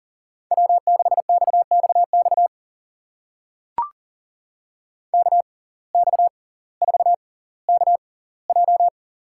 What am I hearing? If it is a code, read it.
Morse code, heard as a keyed tone, about 38 wpm, spaced as for 20 wpm, 700 Hz. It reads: W6XXX KX4KJ
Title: USA Call Signs - 5 Characters - 6x Character Spacing - 40wpm